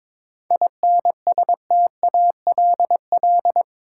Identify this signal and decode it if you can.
Morse code, heard as a keyed tone, about 22 wpm, 700 Hz. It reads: INSTALL
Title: Sentences from Top 1600 Words - New Word Drill - 22wpm